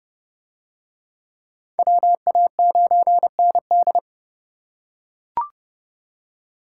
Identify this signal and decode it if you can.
Morse code, heard as a keyed tone, about 30 wpm, 700 Hz. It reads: WA9ND